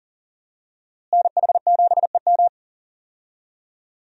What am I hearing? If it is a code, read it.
Morse code, heard as a keyed tone, about 40 wpm, 700 Hz. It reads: NH7EM
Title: USA Call Signs - 5 Characters - Repeated 3x - 40wpm